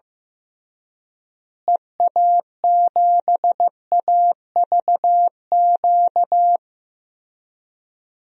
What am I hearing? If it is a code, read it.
Morse code, heard as a keyed tone, about 15 wpm, 700 Hz. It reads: EA7AVQ
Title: DXCC-100 Call Signs - 6 Characters - Repeated 3x - 15wpm